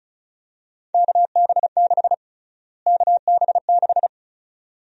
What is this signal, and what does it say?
Morse code, heard as a keyed tone, about 34 wpm, 700 Hz. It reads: KB6 KB6
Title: USA Call Sign Prefixes - Repeated 2x - 35wpm